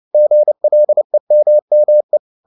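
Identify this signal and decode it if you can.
Morse code, heard as a keyed tone, about 29 wpm, 600 Hz. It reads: GLEMME
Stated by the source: Fast (QRQ) Morse at 29 wpm, from CW med Gnister.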